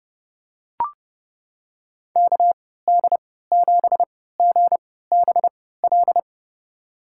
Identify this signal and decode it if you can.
Morse code, heard as a keyed tone, about 32 wpm, spaced as for 17 wpm, 700 Hz. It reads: KD7GBL
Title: USA Call Signs - 6 Characters - 3x Character Spacing - 30wpm